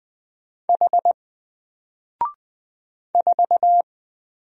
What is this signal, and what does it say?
Morse code, heard as a keyed tone, about 20 wpm, 700 Hz. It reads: H 4
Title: Single Character (Letter:Number:Punctuation:Prosign) 20wpm